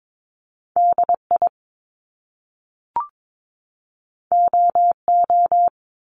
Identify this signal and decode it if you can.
Morse code, heard as a keyed tone, about 22 wpm, 700 Hz. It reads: DI OO
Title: Top Two Letter Combinations in Words 22wpm